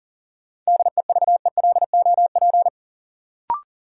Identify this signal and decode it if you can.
Morse code, heard as a keyed tone, about 40 wpm, 700 Hz. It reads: DEVELOP
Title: Sentences from Top 1000 Words - Review Part 2 - New Word Drill - 40wpm